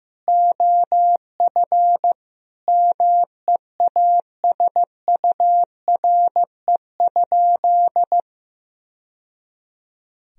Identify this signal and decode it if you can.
Morse code, heard as a keyed tone, about 16 wpm, 700 Hz. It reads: OF MEASURE?
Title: Sentences from Top 1000 Words - Review Part 2 - 15wpm